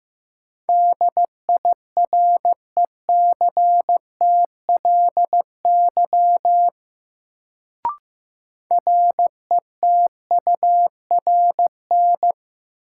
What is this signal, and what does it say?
Morse code, heard as a keyed tone, about 15 wpm, 700 Hz. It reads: DIRECTLY RETURN
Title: Sentences from Top 800 Words - New Word Drill - 15wpm